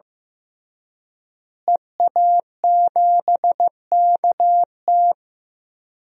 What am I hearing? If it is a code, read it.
Morse code, heard as a keyed tone, about 15 wpm, 700 Hz. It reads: EA7KT
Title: DXCC-100 Call Signs - 5 Characters - Repeated 3x - 15wpm